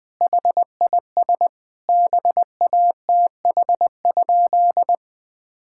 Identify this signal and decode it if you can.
Morse code, heard as a keyed tone, about 20 wpm, 700 Hz. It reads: HIS BATH?
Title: Sentences from Top 1600 Words 20wpm